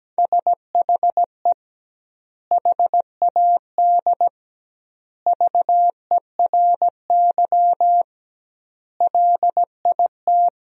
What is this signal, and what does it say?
Morse code, heard as a keyed tone, about 17 wpm, 700 Hz. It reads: SHE HAD VERY LIT
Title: 2x Word-Spacing - Sentences from Top 400 Words - 17wpm